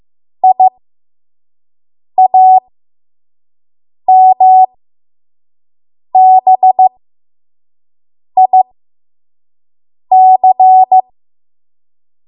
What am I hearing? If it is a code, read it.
Morse code, heard as a keyed tone, about 15 wpm, spaced as for 5 wpm, 750 Hz. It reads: IAMBIC
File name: Morse / 05WPM